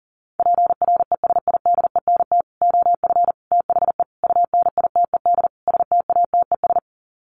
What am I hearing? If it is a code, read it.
Morse code, heard as a keyed tone, about 40 wpm, 700 Hz. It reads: PRESIDENT OF THE UNITED STATES
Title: Sets of 5 Words 40wpm